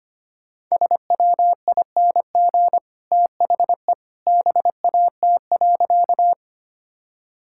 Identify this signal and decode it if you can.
Morse code, heard as a keyed tone, about 25 wpm, 700 Hz. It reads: SWING THE BAT.
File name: Easy Sentences with Hints - 25wpm